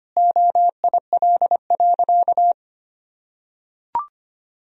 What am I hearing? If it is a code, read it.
Morse code, heard as a keyed tone, about 25 wpm, 700 Hz. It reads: OIL.
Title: Sentences from Top 1000 Words - Review Part 4 - 25wpm